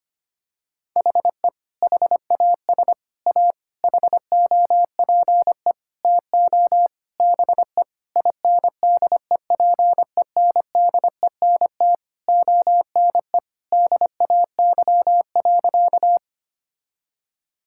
Morse code, heard as a keyed tone, about 25 wpm, 700 Hz. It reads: HE HAS A HOPE TO BE INDEPENDENT ONE DAY.
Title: Sentences from Top 1000 Words - Review Part 6 - 25wpm